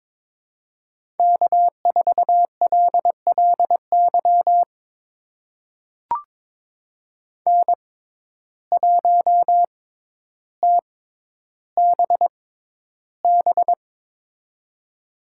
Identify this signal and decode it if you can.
Morse code, heard as a keyed tone, about 22 wpm, spaced as for 11 wpm, 700 Hz. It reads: K4LLY N1TBB
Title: USA Call Signs - 5 Characters - 6x Character Spacing - 22wpm